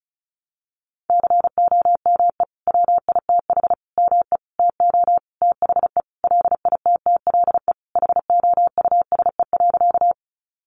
Morse code, heard as a keyed tone, about 34 wpm, 700 Hz. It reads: COME WITH ME TO THE LITTLE HOUSE.
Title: Sentences from Top 200 Words 35wpm